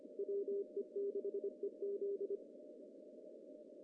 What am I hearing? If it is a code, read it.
Morse code, heard as a keyed tone, about 26 wpm, 400 Hz. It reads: WE6EZ
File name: CW / cqww-cw-k5tr